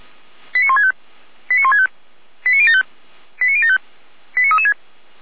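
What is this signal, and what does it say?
Five-tone selective call (ZVEI), heard as quick runs of five tones, tones of about 80 ms. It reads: 89165 89165 89075 89075 89336
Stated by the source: ZVEI five-tone call received by SDR, from the feuerwehr-orlamuende.org fire brigade audio server, station f85360.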